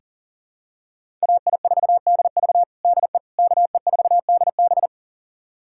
Morse code, heard as a keyed tone, about 37 wpm, 700 Hz. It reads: AI4DV DE KE4DB
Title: USA Callsign DE Callsign - 5 Characters - 40wpm